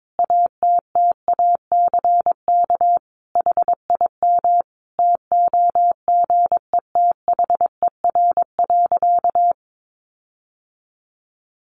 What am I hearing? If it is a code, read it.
Morse code, heard as a keyed tone, about 22 wpm, 700 Hz. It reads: ATTACK HIM TOGETHER.
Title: Sentences from Top 1500 Words 22wpm